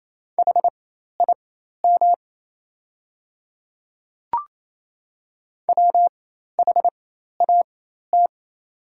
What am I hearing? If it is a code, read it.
Morse code, heard as a keyed tone, about 28 wpm, spaced as for 13 wpm, 690 Hz. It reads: HIM WHAT
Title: 4x Character Spacing - Top 100 Words - 28wpm